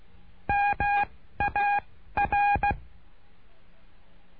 Morse code, heard as a keyed tone, about 16 wpm, 820 Hz. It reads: M A R
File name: cursos / CURSO DE TELEGRAFIA / Telegrafia - Trilha 09